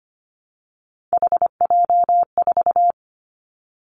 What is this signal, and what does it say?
Morse code, heard as a keyed tone, about 25 wpm, 700 Hz. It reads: HJ4
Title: Callsign Prefixes for Top 100 DXCC - 25wpm